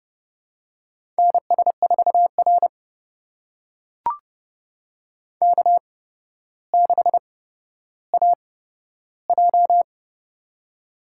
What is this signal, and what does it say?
Morse code, heard as a keyed tone, about 29 wpm, spaced as for 13 wpm, 700 Hz. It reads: NS4R K6AJ
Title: USA Call Signs - 4 Characters - 8x Character Spacing - 30wpm